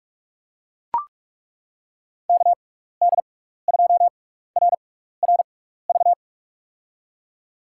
Morse code, heard as a keyed tone, about 42 wpm, spaced as for 15 wpm, 700 Hz. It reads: KD2RRV
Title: USA Call Signs - 6x Character Spacing - 45wpm